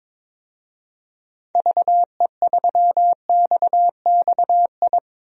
Morse code, heard as a keyed tone, about 22 wpm, 700 Hz. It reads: VE3XXI
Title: DXCC-100 Call Signs - 6 Characters - 22wpm